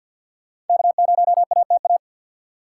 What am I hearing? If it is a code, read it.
Morse code, heard as a keyed tone, about 51 wpm, 700 Hz. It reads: K0ATA
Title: USA Call Signs - 5 Characters - 3x Character Spacing - 50wpm